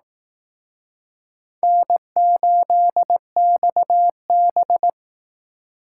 Morse code, heard as a keyed tone, about 18 wpm, 700 Hz. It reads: N8XB
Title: Speed Racing - USA Call Signs 15wpm